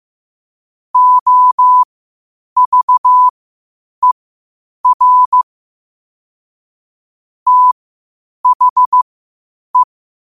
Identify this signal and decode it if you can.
Morse code, heard as a keyed tone, about 15 wpm, spaced as for 8 wpm, 1000 Hz. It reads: OVER THE
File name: qotdfiles / 8wpm